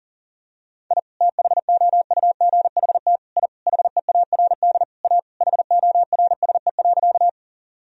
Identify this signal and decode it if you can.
Morse code, heard as a keyed tone, about 40 wpm, 700 Hz. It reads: I THOUGHT I HEARD A HORSE.
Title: Sentences from Top 1000 Words - Review Part 1 - 40wpm